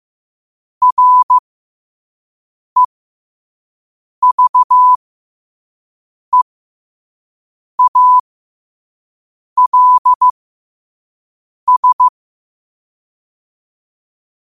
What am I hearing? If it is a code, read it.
Morse code, heard as a keyed tone, about 15 wpm, spaced as for 5 wpm, 1000 Hz. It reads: REVEALS
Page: New quotes every day in morse code at 5 Words per minute.